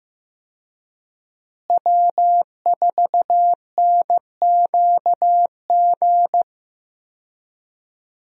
Morse code, heard as a keyed tone, about 15 wpm, 700 Hz. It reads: W4NQG
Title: USA Call Signs - 5 Characters - Repeated 2x - 15wpm